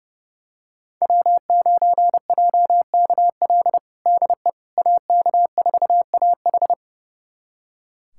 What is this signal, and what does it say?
Morse code, heard as a keyed tone, about 32 wpm, 700 Hz. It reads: W9JKL DE AK4AH
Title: USA Callsign DE Callsign 30wpm